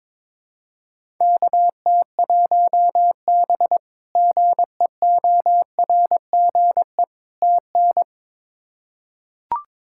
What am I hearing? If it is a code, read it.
Morse code, heard as a keyed tone, about 22 wpm, 700 Hz. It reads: KT1B GEORGE TN
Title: K1USN SST QSO Practice - 22wpm